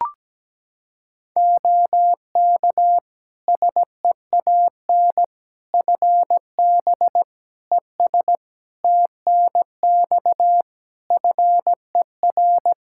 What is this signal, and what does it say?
Morse code, heard as a keyed tone, about 17 wpm, 700 Hz. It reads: OK SEAN FB ES TNX FER
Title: Practice QSO - Exchange 2 - Info - 17wpm